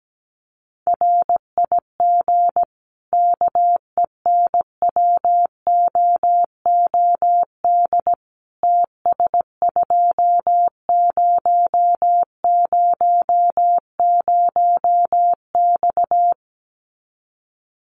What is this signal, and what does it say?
Morse code, heard as a keyed tone, about 17 wpm, 700 Hz. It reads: RIG KENWOOD TS2000X
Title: Common QSO Phrases 17wpm